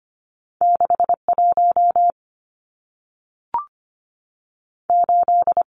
Morse code, heard as a keyed tone, about 25 wpm, spaced as for 3 wpm, 700 Hz. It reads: 6 18
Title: Numbers - 2-Digit 25wpm